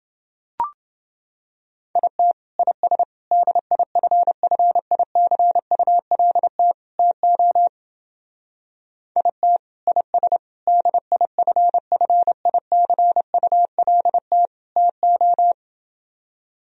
Morse code, heard as a keyed tone, about 28 wpm, 700 Hz. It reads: IT IS DIFFICULT TO IT IS DIFFICULT TO